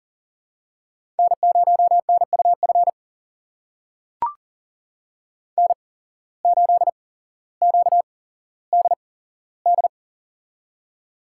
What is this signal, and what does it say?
Morse code, heard as a keyed tone, about 41 wpm, spaced as for 17 wpm, 700 Hz. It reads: N0NUF N8QDD
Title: USA Call Signs - 5 Characters - 8x Character Spacing - 40wpm